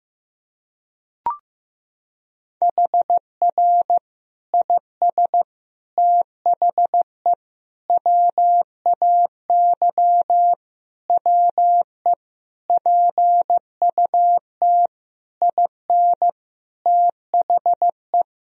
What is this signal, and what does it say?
Morse code, heard as a keyed tone, about 15 wpm, 700 Hz. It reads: HR IS THE WAY WE PUT IN THE
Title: Sentences from Top 200 Words and Abbreviations 15wpm